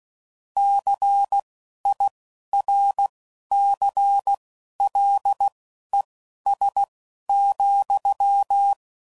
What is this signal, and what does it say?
Morse code, heard as a keyed tone, about 16 wpm, spaced as for 11 wpm, 800 Hz. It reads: CIRCLES,
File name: CW Code Groups @10WPM / Technical Subjects / XC